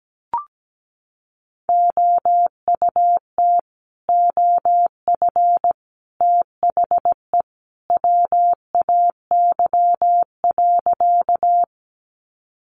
Morse code, heard as a keyed tone, about 17 wpm, 700 Hz. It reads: OUT OF THE WAY.
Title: Sentences from Top 100 Words 17wpm